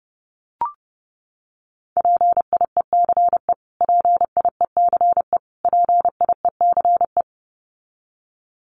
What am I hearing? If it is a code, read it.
Morse code, heard as a keyed tone, about 30 wpm, 700 Hz. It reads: PIECE PIECE PIECE
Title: Top 500 Words Repeated 30wpm